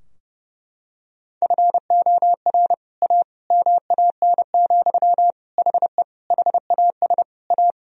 Morse code, heard as a keyed tone, about 30 wpm, 700 Hz. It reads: FOR A MAN, HE HAS A
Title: Sentences from Top 1000 Words - Review Part 2 - 30wpm